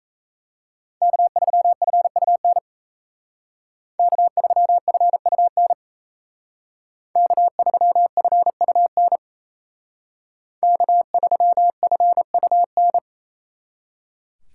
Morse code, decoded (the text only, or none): K3FUN K3FUN K3FUN K3FUN